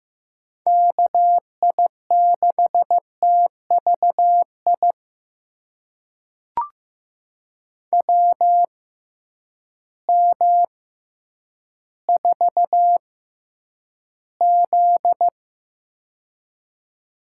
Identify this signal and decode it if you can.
Morse code, heard as a keyed tone, about 15 wpm, spaced as for 9 wpm, 700 Hz. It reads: KI6TVI WM4Z
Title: USA Call Signs - 6x Character Spacing - 15wpm